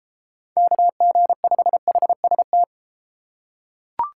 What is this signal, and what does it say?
Morse code, heard as a keyed tone, about 33 wpm, 700 Hz. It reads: KG5HST